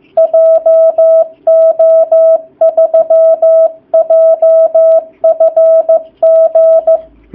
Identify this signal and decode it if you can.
Morse code, heard as a keyed tone, about 15 wpm, 660 Hz. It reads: JO3JFG
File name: jh8cbh / 2019cw / sp15- w